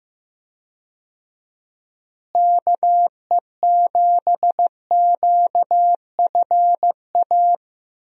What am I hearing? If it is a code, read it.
Morse code, heard as a keyed tone, about 15 wpm, 700 Hz. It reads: KE7QFA
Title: USA Call Signs - 6 Characters - Repeated 2x - 15wpm